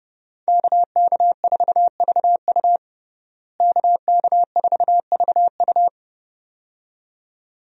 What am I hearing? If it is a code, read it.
Morse code, heard as a keyed tone, about 30 wpm, 700 Hz. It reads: KK4VU KK4VU
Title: USA Call Signs - 5 Characters - Repeated 2x - 30wpm